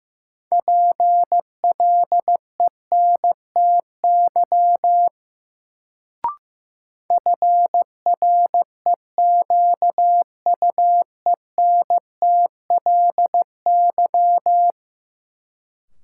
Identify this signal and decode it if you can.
Morse code, heard as a keyed tone, about 15 wpm, 700 Hz. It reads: PLENTY FREQUENTLY
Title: Sentences from Top 1300 Words - New Word Drill Alt - 15wpm